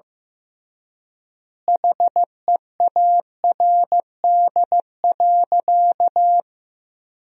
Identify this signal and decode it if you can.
Morse code, heard as a keyed tone, about 16 wpm, 700 Hz. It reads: HEARD.
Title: 3x Word-Spacing - Sentences from Top 400 Words - 15wpm